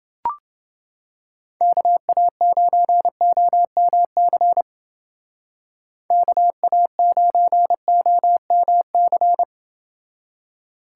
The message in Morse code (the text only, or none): KA9OMC KA9OMC